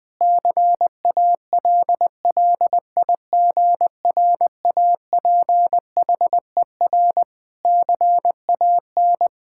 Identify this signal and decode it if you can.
Morse code, heard as a keyed tone, about 21 wpm, 700 Hz. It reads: CALLIGRAPHER CAN
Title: Pangram Challenge v2 - 20wpm